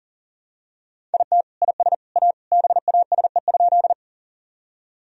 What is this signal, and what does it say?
Morse code, heard as a keyed tone, about 40 wpm, 700 Hz. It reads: IT IS A BASE?
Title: Sentences from Top 1000 Words - Review Part 1 - 40wpm